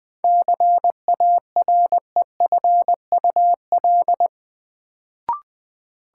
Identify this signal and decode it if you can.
Morse code, heard as a keyed tone, about 20 wpm, 700 Hz. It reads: CAREFUL
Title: Sentences from Top 1600 Words - New Word Drill Alt - 20wpm